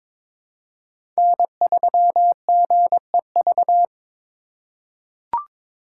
Morse code, heard as a keyed tone, about 22 wpm, 700 Hz. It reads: N3GEV
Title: Code Groups - 5 Letter:Numbers - 22wpm